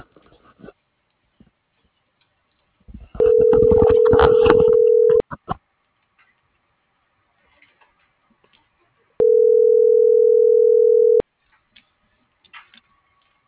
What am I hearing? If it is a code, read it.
Ringback tone (North America), 440 + 480 Hz.